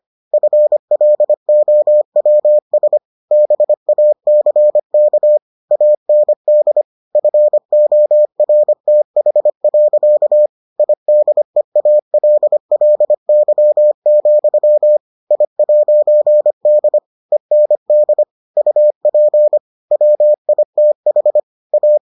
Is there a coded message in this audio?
Yes: Morse code from 0.3 to 22.0 s, heard as a keyed tone, about 25 wpm, 600 Hz. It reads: FLOWS BACK AND FORTH. IDEALLY, I'D END UP WITH A